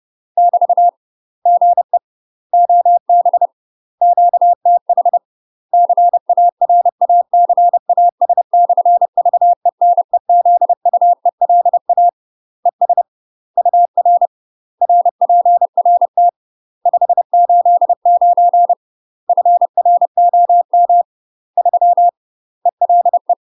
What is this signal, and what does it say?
Morse code, heard as a keyed tone, about 30 wpm, 700 Hz. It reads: <BT> GE OB QTH CARACAS/VENEZUELA ES UR RPRT 589 FROM 3 ELE
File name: UK MASTER QSOs TRAINING / 30WPM / UK MASTER QSOs 0022